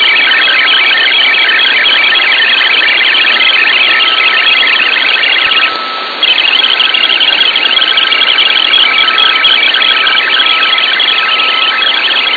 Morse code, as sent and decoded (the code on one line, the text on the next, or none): none